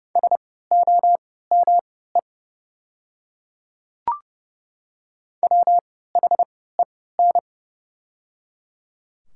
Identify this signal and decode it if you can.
Morse code, heard as a keyed tone, about 30 wpm, spaced as for 17 wpm, 700 Hz. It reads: SOME WHEN
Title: 3x Character Spacing - Top 200 Words - 30wpm